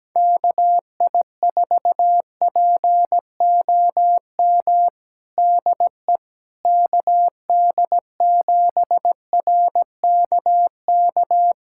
Morse code, heard as a keyed tone, about 17 wpm, 700 Hz. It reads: KI4POM DE KD7RKK
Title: USA Callsign DE Callsign - 6 Characters - 17wpm